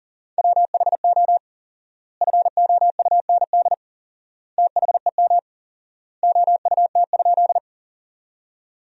Morse code, heard as a keyed tone, about 38 wpm, 700 Hz. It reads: WHO FOUND THEM OUT?
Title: 4x Word-Spacing - Sentences from Top 300 Words - 40wpm